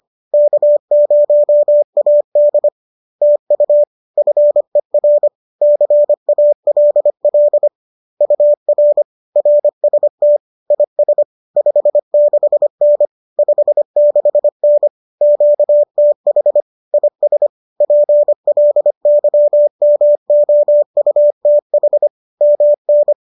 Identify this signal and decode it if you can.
Morse code, heard as a keyed tone, about 25 wpm, 590 Hz. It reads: K0AD TU FER CALL UR RST IS 56N 56N QTH IS PLYMOUTH MN
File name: qso213 25